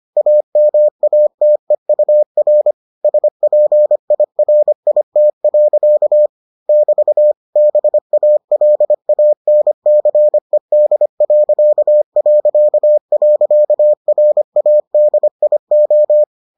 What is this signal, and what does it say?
Morse code, heard as a keyed tone, about 25 wpm, 600 Hz. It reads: AMATEUR SPIRIT. <BT> BALANCED...RADIO